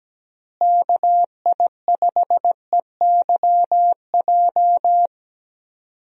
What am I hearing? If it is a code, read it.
Morse code, heard as a keyed tone, about 17 wpm, 700 Hz. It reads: KI5EYJ
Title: USA Call Signs - 6 Characters - Repeated 2x - 17wpm